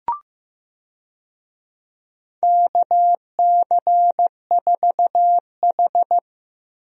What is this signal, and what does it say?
Morse code, heard as a keyed tone, about 15 wpm, 700 Hz. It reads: KC4H
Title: USA Call Signs - 4 Characters - Repeated 2x - 15wpm